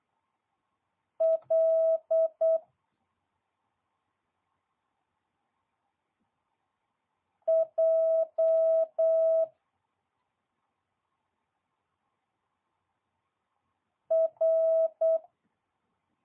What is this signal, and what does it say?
Morse code, heard as a keyed tone, about 8 wpm, 650 Hz. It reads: L J R
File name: jh8cbh / 2022cw